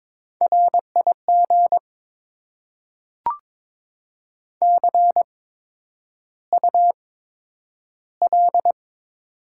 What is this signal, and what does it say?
Morse code, heard as a keyed tone, about 22 wpm, spaced as for 9 wpm, 700 Hz. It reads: RIG CUL
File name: Top 90 QSO Elements - 8x Character Spacing - 22wpm